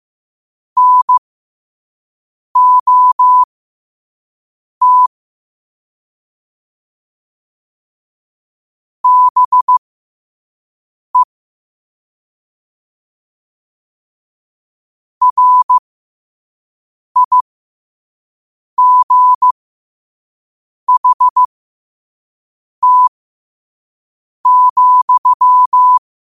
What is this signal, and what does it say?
Morse code, heard as a keyed tone, about 15 wpm, spaced as for 5 wpm, 1000 Hz. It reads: NOT BE RIGHT,
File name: qotdfiles / 5wpm